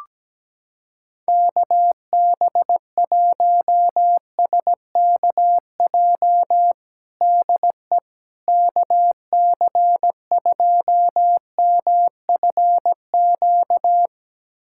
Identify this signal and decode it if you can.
Morse code, heard as a keyed tone, about 17 wpm, 700 Hz. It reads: KB1SKJ DE KC2MFQ
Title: USA Callsign DE Callsign 17wpm